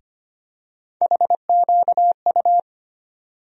Morse code, heard as a keyed tone, about 25 wpm, 700 Hz. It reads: HQU
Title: USA Call Sign Postfixes - 3 Letters 25wpm